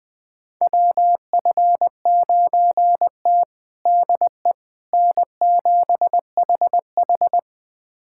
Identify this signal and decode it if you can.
Morse code, heard as a keyed tone, about 21 wpm, 700 Hz. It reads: WF9T DE N7HH
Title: USA Callsign DE Callsign - 4 Characters - 20wpm